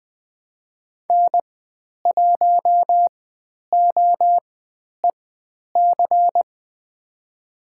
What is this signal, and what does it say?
Morse code, heard as a keyed tone, about 20 wpm, spaced as for 10 wpm, 700 Hz. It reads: N1OEC
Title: 109 Lesson 18g - Letter m - Call Signs 20wpm